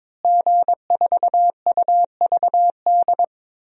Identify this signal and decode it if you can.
Morse code, heard as a keyed tone, about 22 wpm, 700 Hz. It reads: G4UVD
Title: DXCC-100 Call Signs - 5 Characters - Repeated 2x - 22wpm